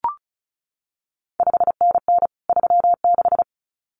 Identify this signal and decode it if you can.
Morse code, heard as a keyed tone, about 36 wpm, 700 Hz. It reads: 5NN 36
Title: CQ WW DX Contest Exchange Encore - 35wpm